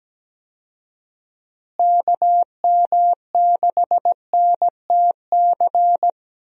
Morse code, heard as a keyed tone, about 17 wpm, 700 Hz. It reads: KM6NTC
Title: USA Call Signs - 6 Characters - Repeated 3x - 17wpm